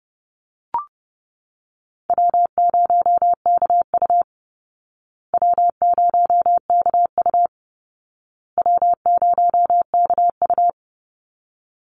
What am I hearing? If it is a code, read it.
Morse code, heard as a keyed tone, about 30 wpm, 700 Hz. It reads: W0KU W0KU W0KU
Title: Easy Call Signs Repeated 3x - 30wpm